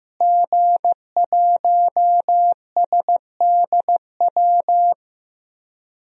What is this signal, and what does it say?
Morse code, heard as a keyed tone, about 15 wpm, 700 Hz. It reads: G1SDW
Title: DXCC-100 Call Signs - 5 Characters - Repeated 3x - 15wpm